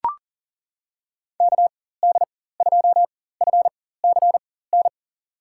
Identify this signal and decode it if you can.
Morse code, heard as a keyed tone, about 40 wpm, spaced as for 19 wpm, 700 Hz. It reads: KD2FCN
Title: USA Call Signs - 6 Characters - 4x Character Spacing - 40wpm